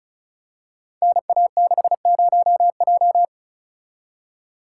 Morse code, heard as a keyed tone, about 34 wpm, 700 Hz. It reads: NA60J
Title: Code Groups - 5 Letter:Numbers - 35wpm